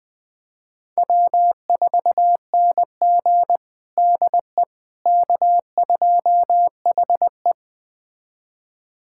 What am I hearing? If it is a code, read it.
Morse code, heard as a keyed tone, about 21 wpm, 700 Hz. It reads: W4NG DE K2HE
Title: USA Callsign DE Callsign - 4 Characters - 20wpm